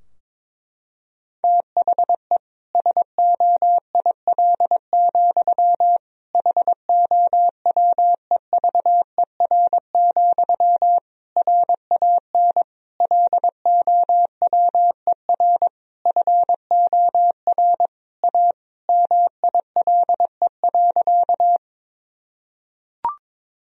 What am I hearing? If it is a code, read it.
Morse code, heard as a keyed tone, about 22 wpm, 700 Hz. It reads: THE SOIL, HOWEVER, RAN LOWER FOR A MILE.
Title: Sentences from Top 1000 Words - Review Part 4 - 22wpm